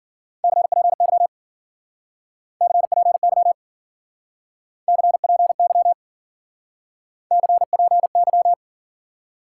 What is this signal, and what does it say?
Morse code, heard as a keyed tone, about 48 wpm, 700 Hz. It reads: CPY CPY CPY CPY
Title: Speed-Racing Top 90 QSO Elements 40wpm